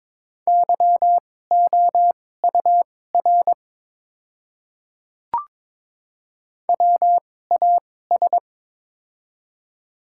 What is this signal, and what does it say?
Morse code, heard as a keyed tone, about 22 wpm, spaced as for 16 wpm, 700 Hz. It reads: YOUR WAS
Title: 2x Character Spacing - Top 100 Words - 22wpm